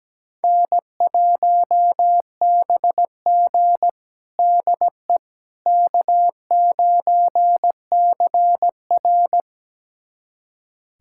Morse code, heard as a keyed tone, about 17 wpm, 700 Hz. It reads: N1BG DE K9CR
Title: USA Callsign DE Callsign - 4 Characters - 17wpm